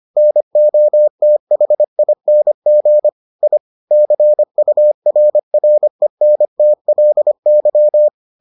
Morse code, heard as a keyed tone, about 25 wpm, 600 Hz. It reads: NOTHING I CURRENTLY